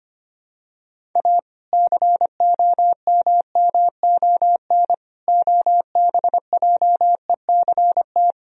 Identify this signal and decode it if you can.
Morse code, heard as a keyed tone, about 25 wpm, 700 Hz. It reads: A COMMON OBJECT
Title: Sentences from Top 1000 Words - Review Part 3 - 25wpm